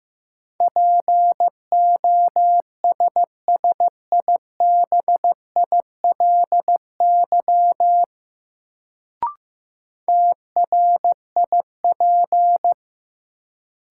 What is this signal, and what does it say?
Morse code, heard as a keyed tone, about 15 wpm, 700 Hz. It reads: POSSIBILY TRIP
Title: Sentences from Top 1300 Words - New Word Drill Alt - 15wpm